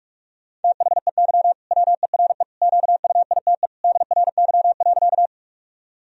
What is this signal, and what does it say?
Morse code, heard as a keyed tone, about 45 wpm, 700 Hz. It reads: THEY WERE QUITE DRY.
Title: Sentences from Top 1000 Words - Review Part 3 - 45wpm